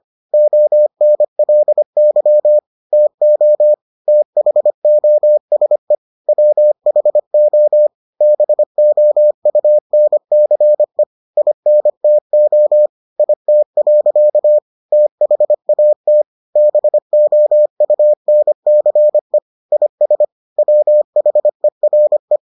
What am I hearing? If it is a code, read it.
Morse code, heard as a keyed tone, about 25 wpm, 600 Hz. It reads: ONLY TO THOSE WHO BOUNCE INTO IT. THAT BOUNCE IS WHERE